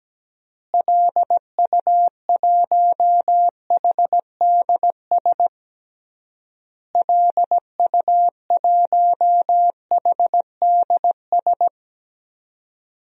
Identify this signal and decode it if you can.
Morse code, heard as a keyed tone, about 17 wpm, 700 Hz. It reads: LU1HDS LU1HDS
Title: DXCC-100 Call Signs - 6 Characters - Repeated 3x - 17wpm